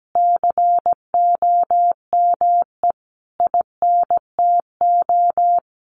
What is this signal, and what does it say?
Morse code, heard as a keyed tone, about 17 wpm, 700 Hz. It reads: COME INTO
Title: Sentences from Top 1300 Words 17wpm